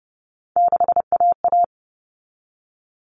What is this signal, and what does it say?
Morse code, heard as a keyed tone, about 30 wpm, 700 Hz. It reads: 6AA
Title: Common Cut Numbers - 3-Digit 30wpm